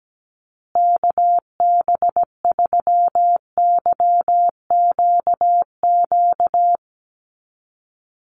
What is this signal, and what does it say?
Morse code, heard as a keyed tone, about 17 wpm, 700 Hz. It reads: KB3YQQ
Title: USA Call Signs Repeated 3x 17wpm